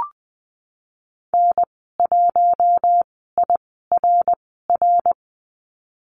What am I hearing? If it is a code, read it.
Morse code, heard as a keyed tone, about 20 wpm, 700 Hz. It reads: N 1 I R R
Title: 055 Lesson 10g - Letter r - Call Signs 20wpm